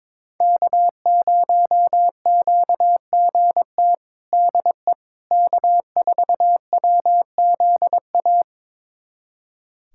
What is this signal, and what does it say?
Morse code, heard as a keyed tone, about 22 wpm, 700 Hz. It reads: K0QGT DE K4WZA
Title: USA Callsign DE Callsign - 5 Characters - 22wpm